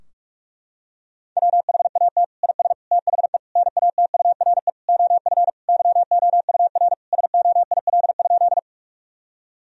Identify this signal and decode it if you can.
Morse code, heard as a keyed tone, about 46 wpm, 700 Hz. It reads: WHAT IS THE NATURE OF YOUR SOIL?
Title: Sentences from Top 1000 Words - Review Part 4 - 45wpm